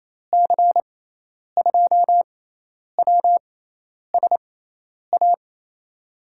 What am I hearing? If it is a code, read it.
Morse code, heard as a keyed tone, about 28 wpm, spaced as for 10 wpm, 700 Hz. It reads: C2WSA
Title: USA Call Signs - 6 Characters - 6x Character Spacing - 28wpm